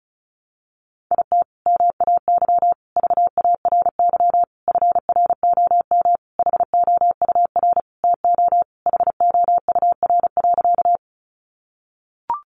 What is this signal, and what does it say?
Morse code, heard as a keyed tone, about 34 wpm, 700 Hz. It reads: IT MAY VARY FROM HOUR TO HOUR.
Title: Sentences from Top 1200 Words 35wpm